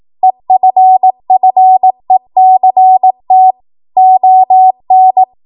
Morse code, heard as a keyed tone, about 18 wpm, 750 Hz. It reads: EFFECT ON
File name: Morse / 18WPM